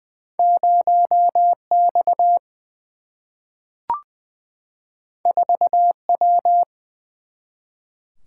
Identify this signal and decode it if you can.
Morse code, heard as a keyed tone, about 20 wpm, 700 Hz. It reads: 0X 4W
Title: Single-Letter Single-Number 20wpm